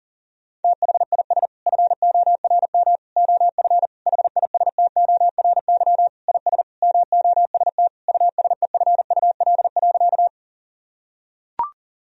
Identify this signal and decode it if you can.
Morse code, heard as a keyed tone, about 40 wpm, 700 Hz. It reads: THIS FORM OF HISTORY IS MOST USEFUL.
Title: Sentences from Top 1000 Words - Review Part 4 - 40wpm